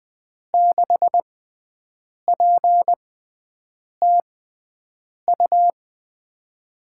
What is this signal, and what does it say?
Morse code, heard as a keyed tone, about 20 wpm, spaced as for 7 wpm, 700 Hz. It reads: 6PTU
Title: USA Call Signs - 6 Characters - 6x Character Spacing - 20wpm